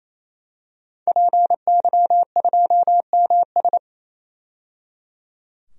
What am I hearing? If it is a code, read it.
Morse code, heard as a keyed tone, about 28 wpm, 700 Hz. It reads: PY2MS
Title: DXCC-100 Call Signs - 5 Characters - 28wpm